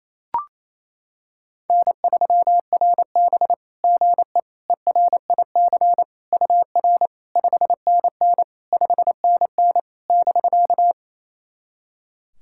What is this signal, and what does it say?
Morse code, heard as a keyed tone, about 28 wpm, 690 Hz. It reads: N3RB GE ERIC UR 5NN 5NN <BK>
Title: SOTA QSO Practice 28wpm